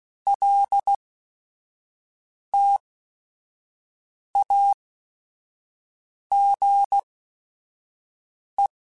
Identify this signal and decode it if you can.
Morse code, heard as a keyed tone, about 16 wpm, spaced as for 5 wpm, 800 Hz. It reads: LTAGE
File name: Technical Subjects / Technical Subjects-mp3s / XC @5